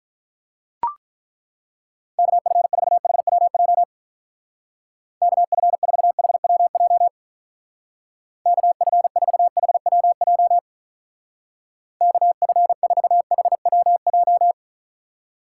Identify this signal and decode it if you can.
Morse code, heard as a keyed tone, about 43 wpm, 700 Hz. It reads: KF4HWJ KF4HWJ KF4HWJ KF4HWJ
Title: Speed Racing - USA 2x3 Call Signs 35wpm